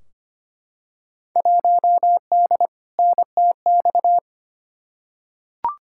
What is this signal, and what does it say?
Morse code, heard as a keyed tone, about 25 wpm, 700 Hz. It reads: 1D NTX
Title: Field Day Exchanges - 25wpm